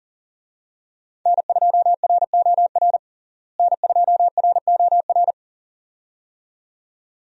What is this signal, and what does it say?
Morse code, heard as a keyed tone, about 40 wpm, 700 Hz. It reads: N2ROR N2ROR
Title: USA Call Signs - 5 Characters - Repeated 2x - 40wpm